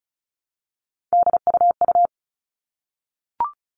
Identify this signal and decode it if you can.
Morse code, heard as a keyed tone, about 34 wpm, 700 Hz. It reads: DUU
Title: All Cut Numbers - 3-Digit 35wpm